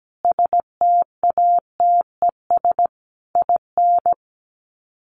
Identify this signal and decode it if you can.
Morse code, heard as a keyed tone, about 17 wpm, 700 Hz. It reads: STATES IN
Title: Sets of 5 Words 17wpm